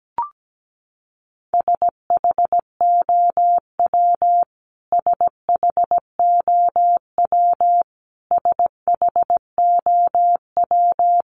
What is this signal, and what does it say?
Morse code, heard as a keyed tone, about 17 wpm, 700 Hz. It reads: SHOW SHOW SHOW
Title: Top 200 Words Repeated 17wpm